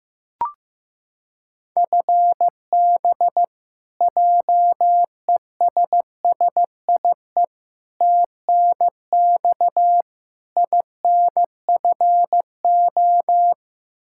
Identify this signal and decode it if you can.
Morse code, heard as a keyed tone, about 15 wpm, 700 Hz. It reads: FB JESSIE TNX INFO
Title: Practice QSO - Exchange 3 - More Info - 15wpm